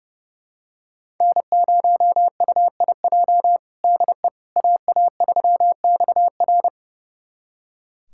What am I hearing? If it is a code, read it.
Morse code, heard as a keyed tone, about 32 wpm, 700 Hz. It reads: N0UIJ DE AA3XR
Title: USA Callsign DE Callsign - 5 Characters - 30wpm